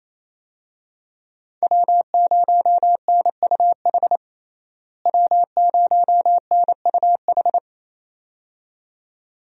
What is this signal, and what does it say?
Morse code, heard as a keyed tone, about 27 wpm, 700 Hz. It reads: W0NUH W0NUH
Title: USA Call Signs - 5 Characters - Repeated 2x - 28wpm